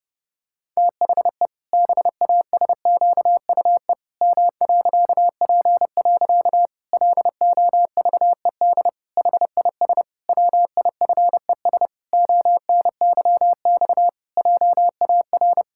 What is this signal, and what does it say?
Morse code, heard as a keyed tone, about 30 wpm, 700 Hz. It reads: THE BASQUE M.P. LOVED HIS WIFES ONYX JAR
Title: Pangram Challenge v2 - 30wpm